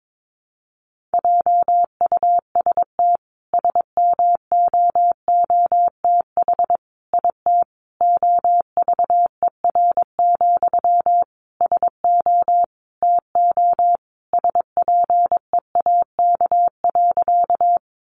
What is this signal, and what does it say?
Morse code, heard as a keyed tone, about 22 wpm, 700 Hz. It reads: JUST SMOOTH IT OVER, SO TO SPEAK.
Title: Sentences from Top 1400 Words 22wpm